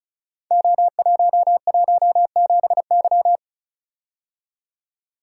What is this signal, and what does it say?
Morse code, heard as a keyed tone, about 36 wpm, 690 Hz. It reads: O117Y
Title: Code Groups - 5 Letter:Numbers - 35wpm